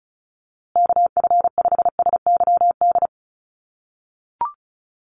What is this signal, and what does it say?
Morse code, heard as a keyed tone, about 34 wpm, 700 Hz. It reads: KF5SYD
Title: USA Call Signs Repeated 2x 35wpm